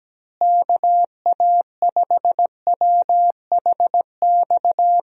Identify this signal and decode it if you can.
Morse code, heard as a keyed tone, about 17 wpm, 700 Hz. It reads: KA5WHX
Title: USA Call Signs - 6 Characters - Repeated 3x - 17wpm